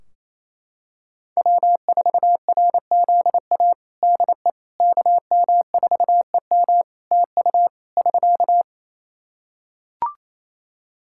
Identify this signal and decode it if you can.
Morse code, heard as a keyed tone, about 28 wpm, 700 Hz. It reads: W4RZA DE KM4EM TU <SK>
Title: Practice QSO - Exchange 4 - Ending - 28wpm